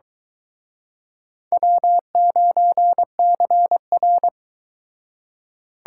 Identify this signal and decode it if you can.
Morse code, heard as a keyed tone, about 23 wpm, 700 Hz. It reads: W9CR